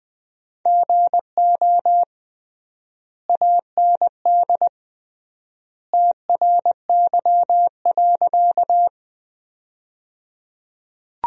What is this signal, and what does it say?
Morse code, heard as a keyed tone, about 21 wpm, 700 Hz. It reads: GO AND TRY.
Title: 3x Word-Spacing - Sentences from Top 200 Words - 20wpm